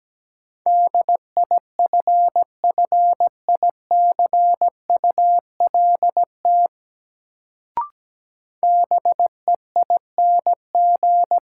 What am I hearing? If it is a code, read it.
Morse code, heard as a keyed tone, about 17 wpm, 700 Hz. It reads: DIFFICULT BEING
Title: Sentences from Top 600 Words - New Word Drill - 17wpm